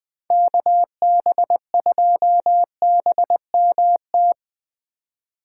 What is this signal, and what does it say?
Morse code, heard as a keyed tone, about 20 wpm, 700 Hz. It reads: KB2BMT
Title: Speed Racing - USA 2x3 Call Signs 17wpm